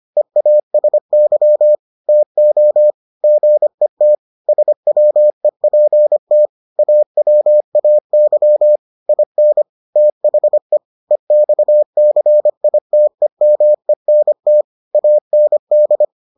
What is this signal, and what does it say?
Morse code, heard as a keyed tone, about 25 wpm, 600 Hz. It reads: EASY TO GET SWEPT AWAY IN THE EXCITEMENT AND